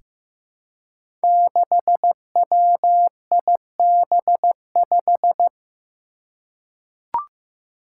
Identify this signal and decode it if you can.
Morse code, heard as a keyed tone, about 15 wpm, 700 Hz. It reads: 6WIB5
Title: Five Challenging Letter-Number Combinations 15wpm